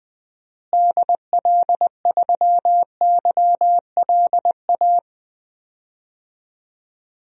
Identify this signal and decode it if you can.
Morse code, heard as a keyed tone, about 21 wpm, 700 Hz. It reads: DL3YLA
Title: DXCC-100 Call Signs - 6 Characters - Repeated 2x - 20wpm